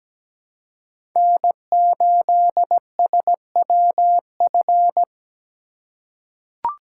Morse code, heard as a keyed tone, about 17 wpm, 700 Hz. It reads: N8SWF
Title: USA Call Signs - 5 Characters - 3x Character Spacing - 17wpm